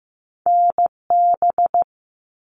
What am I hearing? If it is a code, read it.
Morse code, heard as a keyed tone, about 15 wpm, 700 Hz. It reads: NB
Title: Canadian Provinces and Territories 15wpm